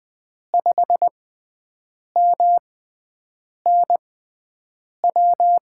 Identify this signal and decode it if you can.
Morse code, heard as a keyed tone, about 20 wpm, spaced as for 7 wpm, 700 Hz. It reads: 5MNW
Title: USA Call Signs - 6 Characters - 6x Character Spacing - 20wpm